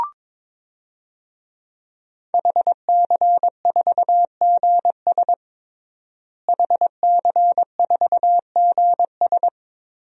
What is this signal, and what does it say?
Morse code, heard as a keyed tone, about 22 wpm, 700 Hz. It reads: HC4GS HC4GS
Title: DXCC-100 Call Signs - 5 Characters - Repeated 3x - 22wpm